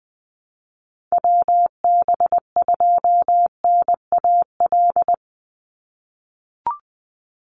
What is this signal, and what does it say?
Morse code, heard as a keyed tone, about 20 wpm, 700 Hz. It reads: WB2NAL
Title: USA Call Signs 20wpm